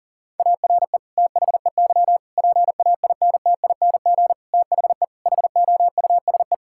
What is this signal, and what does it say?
Morse code, heard as a keyed tone, about 40 wpm, 700 Hz. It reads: ARE THEY PAINTING THE HOUSE
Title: Sentences from Top 1000 Words - Review Part 6 - 40wpm